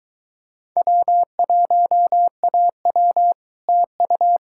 Morse code, heard as a keyed tone, about 26 wpm, 700 Hz. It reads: W1AW TU